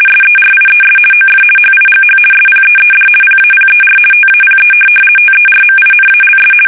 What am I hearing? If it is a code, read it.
RTTY, about 50 baud. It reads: SERIES OF HIGH SPEED PAPER TAPE TERMINALS
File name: fsk850-50B / teletype-corp